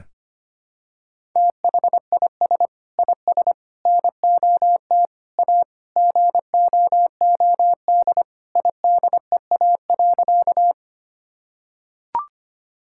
Morse code, heard as a keyed tone, about 25 wpm, 700 Hz. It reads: THIS IS NOT A GOOD IDEA.
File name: practice / 25wpm / Sentences with 4-Word Phrases - 25wpm